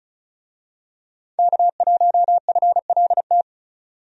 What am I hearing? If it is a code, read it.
Morse code, heard as a keyed tone, about 36 wpm, 700 Hz. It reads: K1FLT
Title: USA Call Signs - 5 Characters - 35wpm